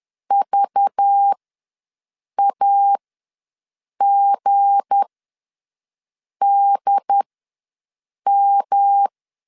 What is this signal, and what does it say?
Morse code, heard as a keyed tone, about 10 wpm, spaced as for 6 wpm, 800 Hz. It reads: VAGDM
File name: LEZIONI / 7 GF9Z / 7 G ws 10wpm 3843